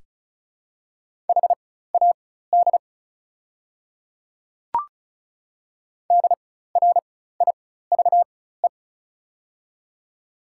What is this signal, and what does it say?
Morse code, heard as a keyed tone, about 36 wpm, spaced as for 16 wpm, 700 Hz. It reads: HAD DRIVE
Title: 4x Character Spacing - Top 500 Words - 35wpm